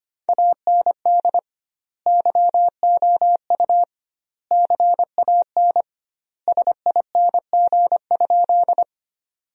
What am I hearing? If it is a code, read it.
Morse code, heard as a keyed tone, about 25 wpm, 700 Hz. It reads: AND YOU CAN SING?
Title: 2x Word-Spacing - Sentences from Top 400 Words - 25wpm